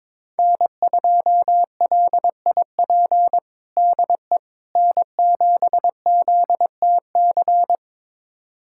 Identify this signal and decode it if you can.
Morse code, heard as a keyed tone, about 22 wpm, 700 Hz. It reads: N2LIP DE N7ZTC
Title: USA Callsign DE Callsign - 5 Characters - 22wpm